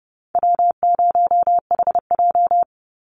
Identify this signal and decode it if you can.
Morse code, heard as a keyed tone, about 30 wpm, 700 Hz. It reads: W0HJ
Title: USA Call Signs 30wpm